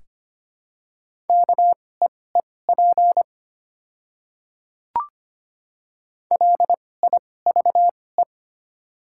Morse code, heard as a keyed tone, about 25 wpm, spaced as for 18 wpm, 700 Hz. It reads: KEEP LIVE
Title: 2x Character Spacing - Top 300 Words - 25wpm